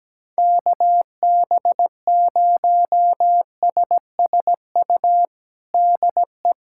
Morse code, heard as a keyed tone, about 17 wpm, 700 Hz. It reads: KB0SSU DE
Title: USA Callsign DE Callsign - 6 Characters - 17wpm